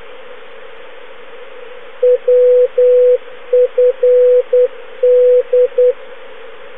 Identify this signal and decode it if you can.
Morse code, heard as a keyed tone, about 10 wpm, 500 Hz. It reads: WFD
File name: europe / g-709-wfd-380 woodford aerodrome, manchester, uk (now closed, alan gale)